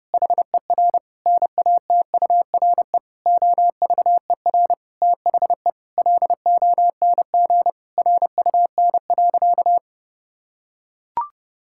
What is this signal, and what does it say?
Morse code, heard as a keyed tone, about 32 wpm, 700 Hz. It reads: HER NATURE OVER THE LONG RUN.
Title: Sentences from Top 1000 Words - Review Part 6 - 30wpm